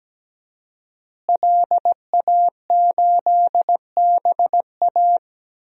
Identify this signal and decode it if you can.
Morse code, heard as a keyed tone, about 17 wpm, 700 Hz. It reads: LA8BA
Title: DXCC-100 Call Signs - 5 Characters - Repeated 2x - 17wpm